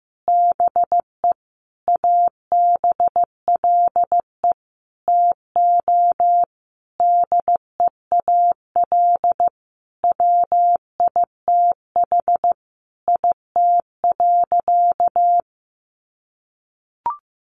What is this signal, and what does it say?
Morse code, heard as a keyed tone, about 14 wpm, 700 Hz. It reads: BE ABLE TO DEAL WITH IT.
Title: Sentences from Top 800 Words 15wpm